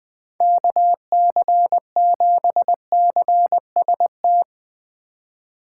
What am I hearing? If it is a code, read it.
Morse code, heard as a keyed tone, about 21 wpm, 700 Hz. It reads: KC7CST
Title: USA Call Signs - 6 Characters - 8x Character Spacing - 20wpm